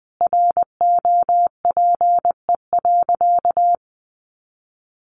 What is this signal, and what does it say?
Morse code, heard as a keyed tone, about 20 wpm, 700 Hz. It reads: ROPE.
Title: Easy Phrases 20wpm